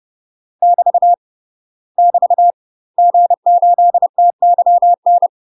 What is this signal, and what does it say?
Morse code, heard as a keyed tone, about 30 wpm, 700 Hz. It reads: <BT> <BT> G8TYN
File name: UK MASTER QSOs TRAINING / 30WPM / UK MASTER QSOs 0050